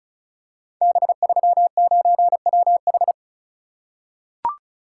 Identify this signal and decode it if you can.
Morse code, heard as a keyed tone, about 36 wpm, 690 Hz. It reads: B39WH
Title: Code Groups - 5 Letter:Numbers - 35wpm